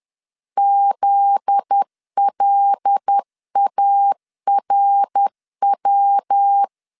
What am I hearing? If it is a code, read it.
Morse code, heard as a keyed tone, about 10 wpm, 800 Hz. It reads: ZLARW
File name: LEZIONI / 7 GF9Z / 7 GF9Z 10wpm 2611